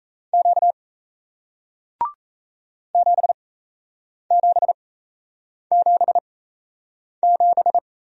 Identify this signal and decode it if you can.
Morse code, heard as a keyed tone, about 38 wpm, spaced as for 8 wpm, 700 Hz. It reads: Q 7777